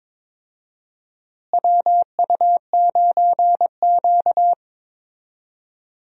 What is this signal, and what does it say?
Morse code, heard as a keyed tone, about 22 wpm, 700 Hz. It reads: WU9Q
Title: USA Call Signs - 4 Characters - Repeated 2x - 22wpm